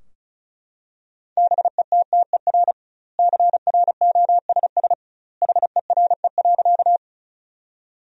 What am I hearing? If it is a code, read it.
Morse code, heard as a keyed tone, about 36 wpm, 700 Hz. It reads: BETTER CROSS HERE.
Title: 2x Word-Spacing - Sentences from Top 400 Words - 35wpm